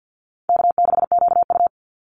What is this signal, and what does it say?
Morse code, heard as a keyed tone, about 50 wpm, 700 Hz. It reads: K6QU
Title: Easy Call Signs Repeated 2x - 50wpm